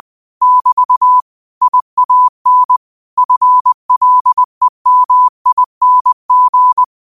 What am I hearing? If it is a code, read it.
Morse code, heard as a keyed tone, about 20 wpm, 1000 Hz. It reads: <BT> IAN FLEMING